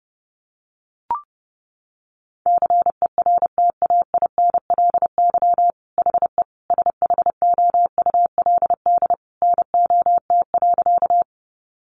Morse code, heard as a keyed tone, about 30 wpm, 700 Hz. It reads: CERTAINLY HE SHOULD NOT.
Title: Sentences from Top 900 Words 30wpm